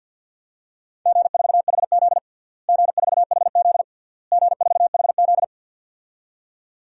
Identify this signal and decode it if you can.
Morse code, heard as a keyed tone, about 52 wpm, 700 Hz. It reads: G4HZ G4HZ G4HZ
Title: DXCC-100 Call Signs - 4 Characters - Repeated 3x - 50wpm